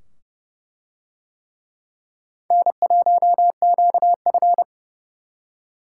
Morse code, heard as a keyed tone, about 27 wpm, 700 Hz. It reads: N1QF